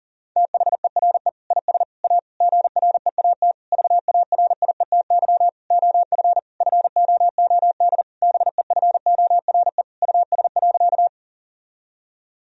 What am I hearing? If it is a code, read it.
Morse code, heard as a keyed tone, about 40 wpm, 690 Hz. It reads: THERE IS A GREAT VARIETY OF FOOD BEFORE US.
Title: Sentences from Top 1000 Words - Review Part 4 - 40wpm